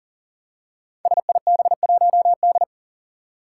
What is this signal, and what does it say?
Morse code, heard as a keyed tone, about 40 wpm, 700 Hz. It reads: SIB1D
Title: Five Challenging Letter-Number Combinations 40wpm